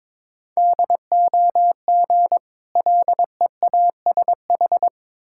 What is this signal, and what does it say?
Morse code, heard as a keyed tone, about 22 wpm, 690 Hz. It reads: DOG LEASH
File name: Difficult Words with Hints - 22wpm